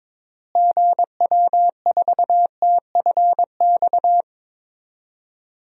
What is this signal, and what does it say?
Morse code, heard as a keyed tone, about 22 wpm, 700 Hz. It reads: GW4TFX
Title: DXCC-100 Call Signs - 6 Characters - Repeated 2x - 22wpm